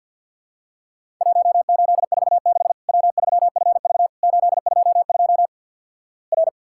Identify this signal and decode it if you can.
Morse code, heard as a keyed tone, about 50 wpm, 700 Hz. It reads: 1846 W3FV 822 R
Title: Worked All Europe DX Contest - QTC Exchange - 50wpm